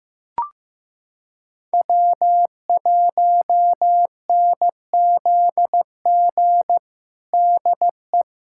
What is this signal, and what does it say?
Morse code, heard as a keyed tone, about 15 wpm, 700 Hz. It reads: W1NZG DE
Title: USA Callsign DE Callsign 15wpm